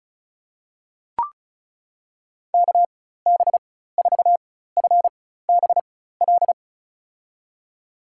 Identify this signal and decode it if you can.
Morse code, heard as a keyed tone, about 36 wpm, spaced as for 16 wpm, 700 Hz. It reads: KB4FBL
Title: USA Call Signs - 6 Characters - 4x Character Spacing - 35wpm